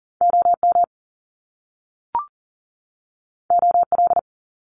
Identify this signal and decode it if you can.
Morse code, heard as a keyed tone, about 40 wpm, 700 Hz. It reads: OM OL
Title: Top Two Letter Combinations in Words 40wpm